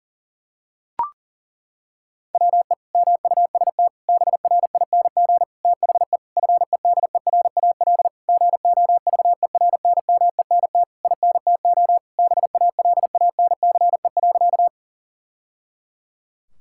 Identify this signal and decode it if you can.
Morse code, heard as a keyed tone, about 40 wpm, 700 Hz. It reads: WE MUST BRING THE FEDERAL GOVERNMENT INTO BALANCE.
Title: Sentences from Top 1000 Words - Review Part 6 - 40wpm